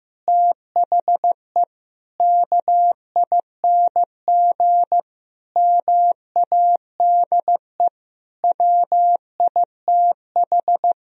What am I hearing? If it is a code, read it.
Morse code, heard as a keyed tone, about 15 wpm, 700 Hz. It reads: THE KING MADE WITH
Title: Sentences from Top 1000 Words - Review Part 4 - 15wpm